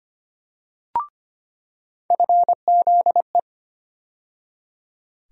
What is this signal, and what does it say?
Morse code, heard as a keyed tone, about 25 wpm, 700 Hz. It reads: FZE
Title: USA Call Sign Postfixes - 3 Letters 25wpm